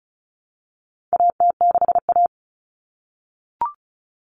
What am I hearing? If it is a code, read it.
Morse code, heard as a keyed tone, about 34 wpm, 700 Hz. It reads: AT6A
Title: Common Cut Numbers - 4-Digit 35wpm